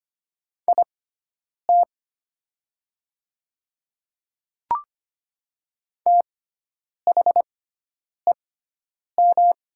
Morse code, heard as a keyed tone, about 25 wpm, spaced as for 9 wpm, 700 Hz. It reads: IT THEM
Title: 6x Character Spacing - Top 100 Words - 25wpm